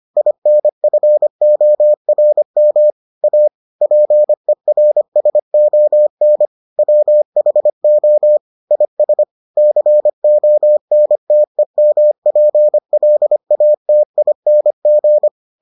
Morse code, heard as a keyed tone, about 25 wpm, 600 Hz. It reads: INFORM A PERSON WHO IS CONTEMPLATING